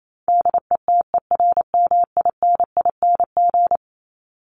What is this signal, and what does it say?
Morse code, heard as a keyed tone, about 28 wpm, 700 Hz. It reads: DETERMINING
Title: ING Suffix - Top 100 Words with Base Words 28wpm